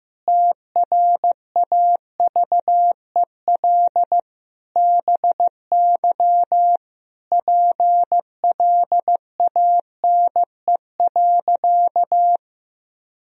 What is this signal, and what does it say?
Morse code, heard as a keyed tone, about 15 wpm, 700 Hz. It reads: TRAVEL BY PLANE.
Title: Sentences from Top 1000 Words - Review Part 3 - 15wpm